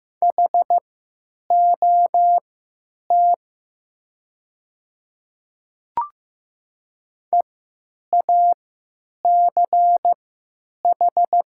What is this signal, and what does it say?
Morse code, heard as a keyed tone, about 15 wpm, spaced as for 9 wpm, 700 Hz. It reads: HOT EACH
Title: 3x Character Spacing - Top 100 Words - 15wpm